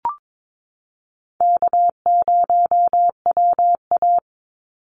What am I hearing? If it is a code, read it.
Morse code, heard as a keyed tone, about 22 wpm, 700 Hz. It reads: K0WA
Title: Easy Call Signs Repeated 2x - 22wpm